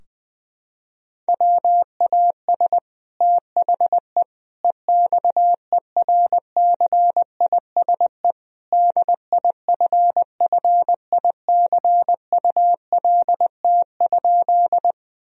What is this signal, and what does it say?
Morse code, heard as a keyed tone, about 21 wpm, 700 Hz. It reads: WAS THE EXERCISE DIFFICULT?
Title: Sentences from Top 1000 Words - Review Part 4 - 20wpm